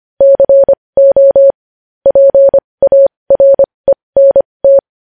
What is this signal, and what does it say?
Morse code, heard as a keyed tone, about 25 wpm, spaced as for 20 wpm, 550 Hz. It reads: CO PARENT
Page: News Headlines in Morse Code at 20 WPM